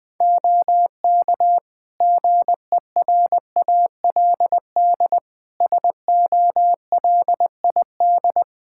Morse code, heard as a keyed tone, about 20 wpm, 700 Hz. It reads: OK GERALD SOLID
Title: Practice QSO - Exchange 3 - More Info - 20wpm